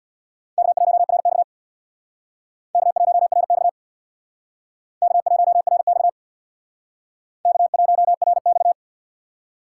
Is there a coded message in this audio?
Yes: Morse code from 0.6 to 8.7 s, heard as a keyed tone, about 63 wpm, 700 Hz. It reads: K1RX K1RX K1RX K1RX